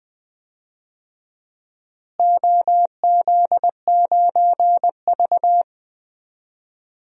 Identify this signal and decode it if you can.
Morse code, heard as a keyed tone, about 20 wpm, 700 Hz. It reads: OZ9V
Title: DXCC-100 Call Signs - 4 Characters - Repeated 2x - 20wpm